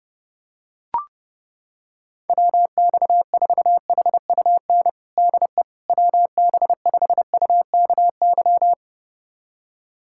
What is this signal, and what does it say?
Morse code, heard as a keyed tone, about 30 wpm, 700 Hz. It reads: WX4HUN DE WB5UKY
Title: USA Callsign DE Callsign - 6 Characters - 30wpm